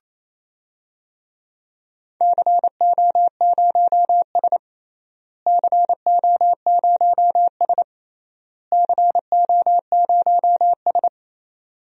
Morse code, heard as a keyed tone, about 27 wpm, 700 Hz. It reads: CO0S CO0S CO0S
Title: DXCC-100 Call Signs - 4 Characters - Repeated 3x - 28wpm